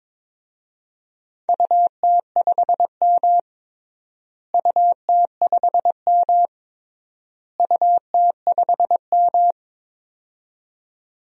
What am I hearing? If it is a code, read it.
Morse code, heard as a keyed tone, about 22 wpm, 700 Hz. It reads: UT5M UT5M UT5M
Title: DXCC-100 Call Signs - 4 Characters - Repeated 3x - 22wpm